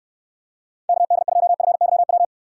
Morse code, heard as a keyed tone, about 69 wpm, 700 Hz. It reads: KD2FYW